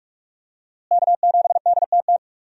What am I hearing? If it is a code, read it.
Morse code, heard as a keyed tone, about 41 wpm, 700 Hz. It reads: K7DTT
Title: USA Call Signs - 5 Characters - Repeated 3x - 45wpm